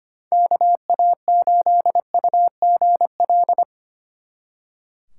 Morse code, heard as a keyed tone, about 32 wpm, 700 Hz. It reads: KA8UGL